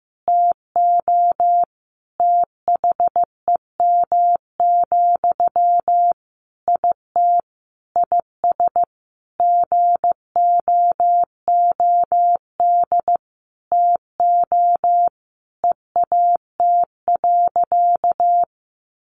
Morse code, heard as a keyed tone, about 15 wpm, 700 Hz. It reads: TO THEM, IT IS GOOD TO EAT.
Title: Sentences from Top 800 Words 15wpm